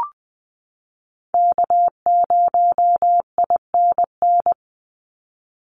Morse code, heard as a keyed tone, about 20 wpm, 700 Hz. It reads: K0INN
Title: Call Signs Repeated 2x - 20wpm